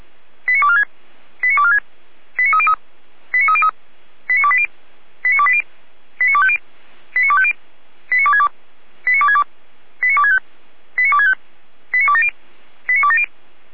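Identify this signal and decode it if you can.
Five-tone selective call (ZVEI), heard as quick runs of five tones, tones of about 70 ms. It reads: 89246 89246 89303 89303 89280 89280 89250 89250 89262 89262 89265 89265 89270 89270
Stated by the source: Five-tone ZVEI selective call received by SDR, archived by Feuerwehr Orlamünde (fire brigade).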